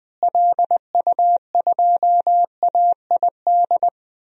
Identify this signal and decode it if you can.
Morse code, heard as a keyed tone, about 20 wpm, 700 Hz. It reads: LU2AID
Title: DXCC-100 Call Signs - 6 Characters - Repeated 3x - 20wpm